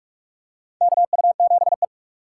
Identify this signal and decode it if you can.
Morse code, heard as a keyed tone, about 46 wpm, 700 Hz. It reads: KU7E
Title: USA Call Signs - 4x Character Spacing - 45wpm